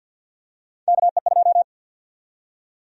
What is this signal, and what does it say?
Morse code, heard as a keyed tone, about 51 wpm, 700 Hz. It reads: KE2
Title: USA Call Sign Prefixes - Repeated 2x - 50wpm